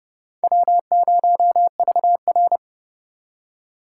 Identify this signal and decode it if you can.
Morse code, heard as a keyed tone, about 38 wpm, 700 Hz. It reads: W0VR